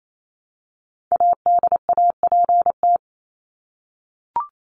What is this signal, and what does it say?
Morse code, heard as a keyed tone, about 28 wpm, 700 Hz. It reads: ADAPT
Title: Top 1501-1750 Words 28wpm